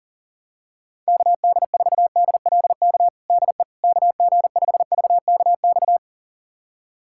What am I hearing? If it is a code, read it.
Morse code, heard as a keyed tone, about 40 wpm, 700 Hz. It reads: KD4DLK DE KG5VKX
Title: USA Callsign DE Callsign - 6 Characters - 40wpm